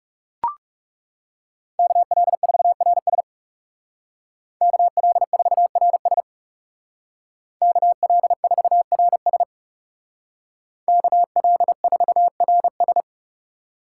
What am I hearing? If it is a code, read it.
Morse code, heard as a keyed tone, about 37 wpm, 700 Hz. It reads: KL4RS KL4RS KL4RS KL4RS
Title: Speed Racing - USA 2x2 Call Signs 30wpm